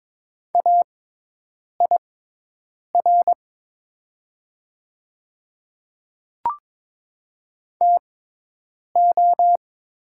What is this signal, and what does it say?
Morse code, heard as a keyed tone, about 22 wpm, spaced as for 8 wpm, 700 Hz. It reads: AIR TO
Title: 6x Character Spacing - Top 200 Words - 22wpm